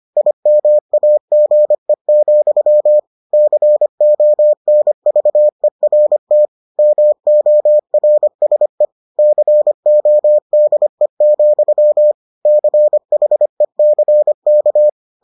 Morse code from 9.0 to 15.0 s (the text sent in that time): CODE, CHECK